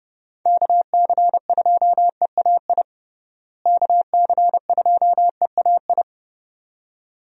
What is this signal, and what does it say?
Morse code, heard as a keyed tone, about 31 wpm, 700 Hz. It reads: KC2EAI KC2EAI
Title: USA Call Signs - 6 Characters - Repeated 2x - 30wpm